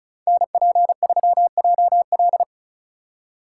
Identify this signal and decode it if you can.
Morse code, heard as a keyed tone, about 43 wpm, 700 Hz. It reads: NP3JL
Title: Speed Racing - USA 2x2 Call Signs 35wpm